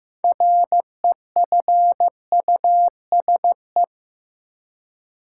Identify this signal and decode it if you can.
Morse code, heard as a keyed tone, about 15 wpm, 700 Hz. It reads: REFUSE
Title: Six Letter Words - 15wpm